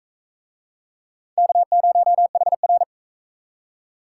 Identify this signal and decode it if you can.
Morse code, heard as a keyed tone, about 41 wpm, 700 Hz. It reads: K0HR